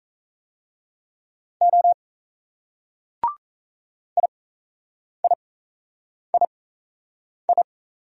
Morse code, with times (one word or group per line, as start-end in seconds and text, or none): 1.6-1.9 O
4.2-7.6 IIII